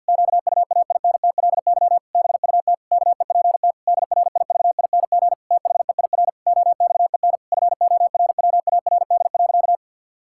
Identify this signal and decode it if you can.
Morse code, heard as a keyed tone, about 50 wpm, 700 Hz. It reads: QUAINTLY BUT KEPT DRIVING THEIR OXEN FORWARD.
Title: Pangram Challenge v2 - 50wpm